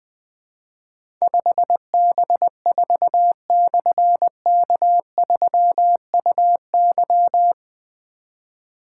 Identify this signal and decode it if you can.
Morse code, heard as a keyed tone, about 20 wpm, 700 Hz. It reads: 5B4/K3UY
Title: Difficult Call Signs 20wpm